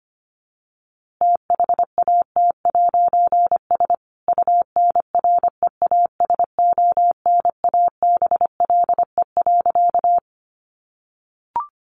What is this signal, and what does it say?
Morse code, heard as a keyed tone, about 25 wpm, 700 Hz. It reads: THAT'S UNREASONABLE.
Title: A Sound Of Thunder 25wpm